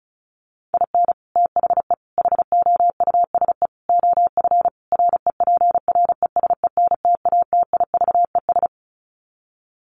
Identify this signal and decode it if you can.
Morse code, heard as a keyed tone, about 34 wpm, 700 Hz. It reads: IN THE HOUSE OF REPRESENTATIVES
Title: Sets of 5 Words Encore 35wpm